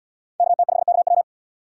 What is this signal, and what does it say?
Morse code, heard as a keyed tone, about 66 wpm, 700 Hz. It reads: KT5KW